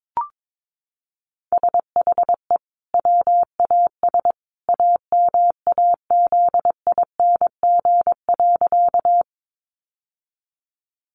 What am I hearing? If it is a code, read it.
Morse code, heard as a keyed tone, about 22 wpm, 700 Hz. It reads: SHE WAS AMAZING.
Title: Sentences from Top 1500 Words 22wpm